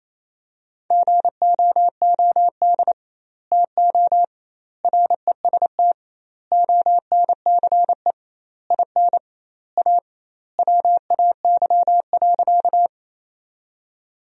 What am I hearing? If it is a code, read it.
Morse code, heard as a keyed tone, about 28 wpm, 700 Hz. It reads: GOOD TO REST ONCE IN A WAY.
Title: 2x Word-Spacing - Sentences from Top 500 Words - 28wpm